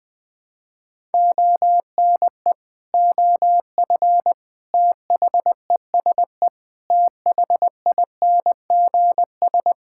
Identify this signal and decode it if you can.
Morse code, heard as a keyed tone, about 21 wpm, 700 Hz. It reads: ONE OF THESE THINGS
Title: Sentences with 8-Word Phrases - Spoken Beginning - 20wpm